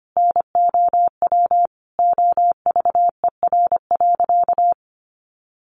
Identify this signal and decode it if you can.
Morse code, heard as a keyed tone, about 25 wpm, 700 Hz. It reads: NOW OVER.
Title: Sentences from Top 1100 Words 25wpm